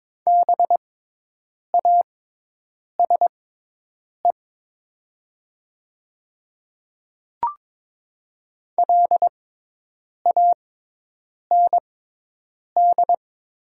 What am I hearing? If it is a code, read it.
Morse code, heard as a keyed tone, about 22 wpm, spaced as for 8 wpm, 700 Hz. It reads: BASE LAND
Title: 6x Character Spacing - Top 300 Words - 22wpm